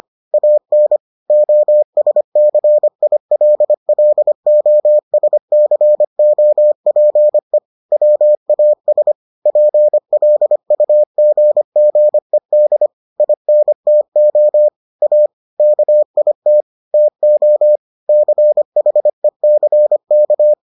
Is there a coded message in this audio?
Yes: Morse code from 0.3 to 20.5 s, heard as a keyed tone, about 25 wpm, 600 Hz. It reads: AN OSCILLOSCOPE WAS PLUGGED INTO A KIT TO CHECK